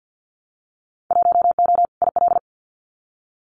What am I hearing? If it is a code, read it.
Morse code, heard as a keyed tone, about 50 wpm, 700 Hz. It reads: 1O IL
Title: Winter Field Day Exchanges 50wpm